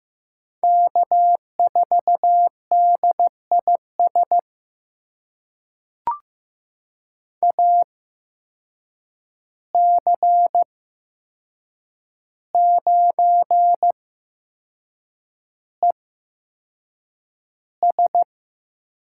Morse code, heard as a keyed tone, about 15 wpm, spaced as for 6 wpm, 700 Hz. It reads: K4DIS AC9ES
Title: USA Call Signs - 8x Character Spacing - 15wpm